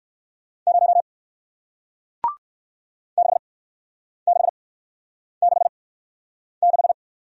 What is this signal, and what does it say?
Morse code, heard as a keyed tone, about 56 wpm, spaced as for 9 wpm, 700 Hz. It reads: 0 6666